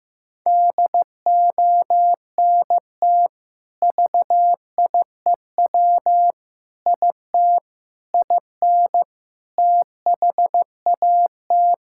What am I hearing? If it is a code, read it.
Morse code, heard as a keyed tone, about 15 wpm, 700 Hz. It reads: DONT VIEW IT IN THAT
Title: Sentences from Top 1000 Words - Review Part 6 - 15wpm